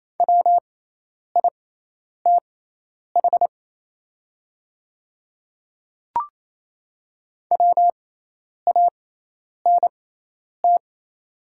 Morse code, heard as a keyed tone, about 28 wpm, spaced as for 10 wpm, 700 Hz. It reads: WITH WANT
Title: 6x Character Spacing - Top 200 Words - 28wpm